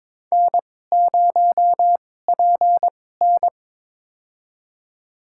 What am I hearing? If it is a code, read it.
Morse code, heard as a keyed tone, about 22 wpm, spaced as for 16 wpm, 700 Hz. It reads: N0PN
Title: USA Call Signs - 4 Characters - 2x Character Spacing - 22wpm